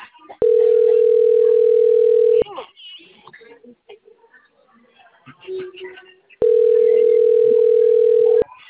Ringback tone (North America), 440 + 480 Hz.
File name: rg-1600-unknown- - - .74756